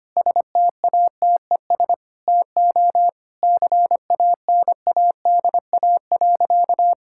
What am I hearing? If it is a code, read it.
Morse code, heard as a keyed tone, about 25 wpm, 700 Hz. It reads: STATES TO CANADA.
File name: practice / 25wpm / Sentences with 4-Word Phrases - 25wpm